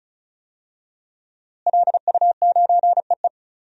Morse code, heard as a keyed tone, about 36 wpm, 700 Hz. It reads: LU9EE
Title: DXCC-100 Call Signs - 5 Characters - Repeated 2x - 35wpm